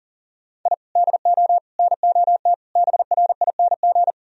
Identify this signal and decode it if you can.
Morse code, heard as a keyed tone, about 40 wpm, 700 Hz. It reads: I DO NOT BRING
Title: Sentences from Top 1000 Words - Review Part 3 - 40wpm